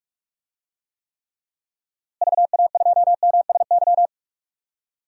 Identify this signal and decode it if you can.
Morse code, heard as a keyed tone, about 43 wpm, 700 Hz. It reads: VA2MSY
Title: DXCC-100 Call Signs - 6 Characters - Repeated 3x - 45wpm